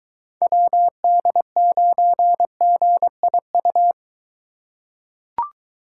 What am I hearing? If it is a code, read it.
Morse code, heard as a keyed tone, about 23 wpm, 700 Hz. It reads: WD9GIU